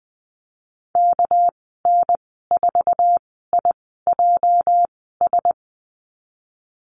Morse code, heard as a keyed tone, about 20 wpm, spaced as for 15 wpm, 700 Hz. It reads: KN4IJS
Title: 219 Lesson 35g - Number 0 - Call Signs 20wpm